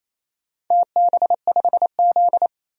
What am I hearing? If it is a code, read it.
Morse code, heard as a keyed tone, about 28 wpm, 700 Hz. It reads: TB5Z
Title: DXCC-100 Call Signs - 4 Characters - 28wpm